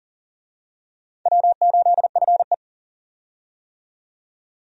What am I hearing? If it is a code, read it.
Morse code, heard as a keyed tone, about 40 wpm, 700 Hz. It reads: W8FE
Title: USA Call Signs - 4 Characters - 40wpm